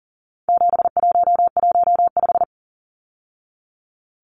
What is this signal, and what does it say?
Morse code, heard as a keyed tone, about 40 wpm, 700 Hz. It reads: 7115
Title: Numbers - 4-Digit 40wpm